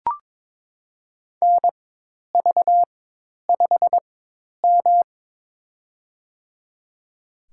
Morse code, heard as a keyed tone, about 22 wpm, spaced as for 10 wpm, 700 Hz. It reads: NV5M
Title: USA Call Signs - 4 Characters - 4x Character Spacing - 22wpm